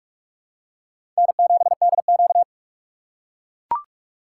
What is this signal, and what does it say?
Morse code, heard as a keyed tone, about 41 wpm, 690 Hz. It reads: N7DQ
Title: USA Call Signs - 4 Characters - 3x Character Spacing - 45wpm